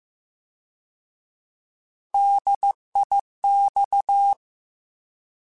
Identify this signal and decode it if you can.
Morse code, heard as a keyed tone, about 15 wpm, 800 Hz. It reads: DIX
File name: Alphabetical / Alphabetical mp3s / Ds @13